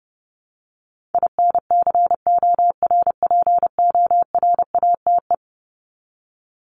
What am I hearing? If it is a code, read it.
Morse code, heard as a keyed tone, about 30 wpm, 700 Hz. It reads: INCORPORATE
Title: Top 1751-2000 Words 30wpm